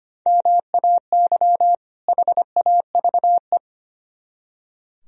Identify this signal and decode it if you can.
Morse code, heard as a keyed tone, about 33 wpm, 700 Hz. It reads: MAY HAVE